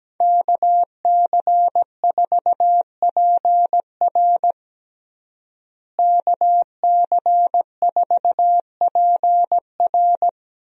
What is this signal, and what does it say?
Morse code, heard as a keyed tone, about 17 wpm, 700 Hz. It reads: KC4PR KC4PR
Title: USA Call Signs - 5 Characters - Repeated 2x - 17wpm